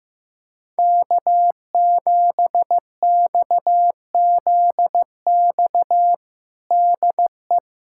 Morse code, heard as a keyed tone, about 15 wpm, 700 Hz. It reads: K7XZX DE
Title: USA Callsign DE Callsign 15wpm